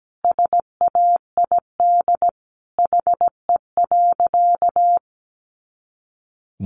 Morse code, heard as a keyed tone, about 17 wpm, 700 Hz. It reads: SAID HE.
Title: Sentences from Top 400 Words 17wpm